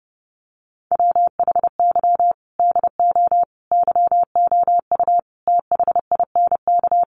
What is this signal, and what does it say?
Morse code, heard as a keyed tone, about 30 wpm, 700 Hz. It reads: WHY DO YOU THINK
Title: Sentences from Top 1400 Words 30wpm